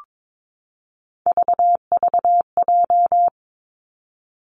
Morse code, heard as a keyed tone, about 22 wpm, 700 Hz. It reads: VVJ
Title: Three Challenging Letter-Number Combinations 22wpm